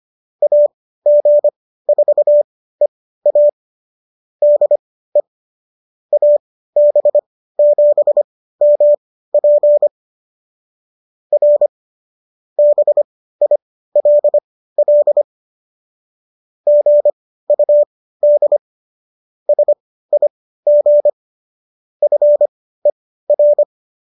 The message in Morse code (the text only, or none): AG4EA DE AB7MP R BILL GUD SIG FER